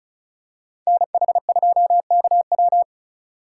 Morse code, heard as a keyed tone, about 34 wpm, 700 Hz. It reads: NH2KW
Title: USA Call Signs - 4x Character Spacing - 35wpm